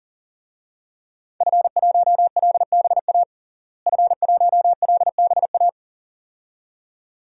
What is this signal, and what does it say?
Morse code, heard as a keyed tone, about 40 wpm, 700 Hz. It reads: F1LBA F1LBA
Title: DXCC-100 Call Signs - 5 Characters - Repeated 2x - 40wpm